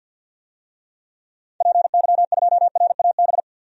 Morse code, heard as a keyed tone, about 52 wpm, 700 Hz. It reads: PY2RAB
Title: DXCC-100 Call Signs - 6 Characters - 50wpm